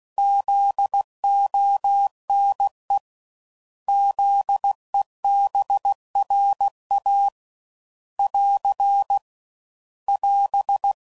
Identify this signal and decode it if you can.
Morse code, heard as a keyed tone, about 16 wpm, 800 Hz. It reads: ZONE ZEBRA <AR> <AS>
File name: Prefixes / By First Letter